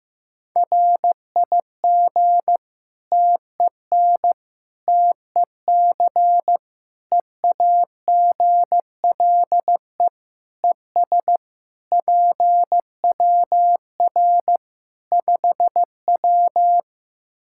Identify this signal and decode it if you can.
Morse code, heard as a keyed tone, about 15 wpm, 700 Hz. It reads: RIG TEN TEC EAGLE ES PWR 5W
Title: Practice QSO - Exchange 2 - Info - 15wpm